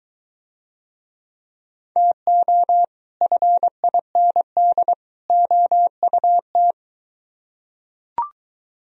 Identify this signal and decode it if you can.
Morse code, heard as a keyed tone, about 23 wpm, 700 Hz. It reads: TO FIND OUT